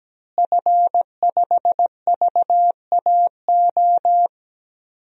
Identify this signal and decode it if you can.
Morse code, heard as a keyed tone, about 17 wpm, 700 Hz. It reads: F5VAO
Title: DXCC-100 Call Signs - 5 Characters - Repeated 3x - 17wpm